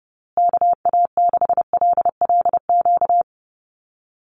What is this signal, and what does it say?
Morse code, heard as a keyed tone, about 30 wpm, 700 Hz. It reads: KA6LLQ
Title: USA Call Signs Repeated 3x 30wpm